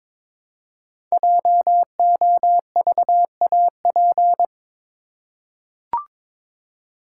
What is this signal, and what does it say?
Morse code, heard as a keyed tone, about 22 wpm, 700 Hz. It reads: JOVAP
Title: Code Groups - 5 Letters - 22wpm